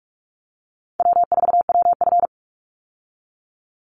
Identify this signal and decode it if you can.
Morse code, heard as a keyed tone, about 44 wpm, 700 Hz. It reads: W4WF
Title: Call Signs - 45wpm